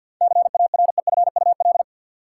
Morse code, heard as a keyed tone, about 52 wpm, 700 Hz. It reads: CAREFUL
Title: Sentences from Top 1600 Words - New Word Drill Alt - 50wpm